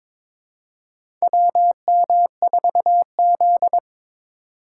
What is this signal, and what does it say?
Morse code, heard as a keyed tone, about 22 wpm, 700 Hz. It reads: WM4Z
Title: USA Call Signs - 4 Characters - 4x Character Spacing - 22wpm